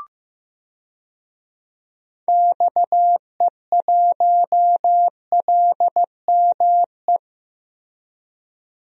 Morse code, heard as a keyed tone, about 15 wpm, 700 Hz. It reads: XE1LME
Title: DXCC-100 Call Signs - 6 Characters - 15wpm